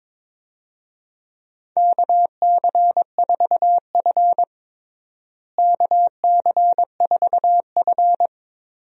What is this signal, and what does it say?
Morse code, heard as a keyed tone, about 23 wpm, 700 Hz. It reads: KC4F KC4F
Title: USA Call Signs - 4 Characters - Repeated 2x - 22wpm